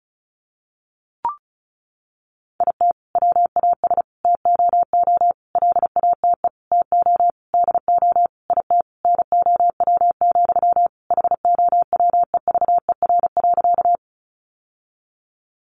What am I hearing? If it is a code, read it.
Morse code, heard as a keyed tone, about 34 wpm, 700 Hz. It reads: IT WAS TOO LATE TO DO IT NOW, HOWEVER.
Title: Sentences from Top 600 Words 35wpm